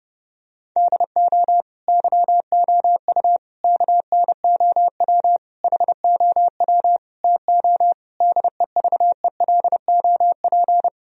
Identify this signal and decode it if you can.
Morse code, heard as a keyed tone, about 32 wpm, 700 Hz. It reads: DO YOU KNOW HOW TO DEVELOP
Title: Sentences from Top 1000 Words - Review Part 3 - 30wpm